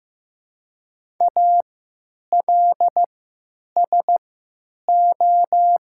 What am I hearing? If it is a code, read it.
Morse code, heard as a keyed tone, about 15 wpm, spaced as for 9 wpm, 700 Hz. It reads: ALSO
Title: 3x Character Spacing - Top 500 Words - 15wpm